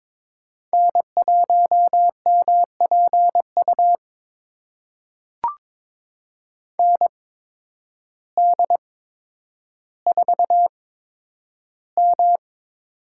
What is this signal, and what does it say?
Morse code, heard as a keyed tone, about 22 wpm, spaced as for 10 wpm, 700 Hz. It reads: N1MPU ND4M
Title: USA Call Signs - 8x Character Spacing - 22wpm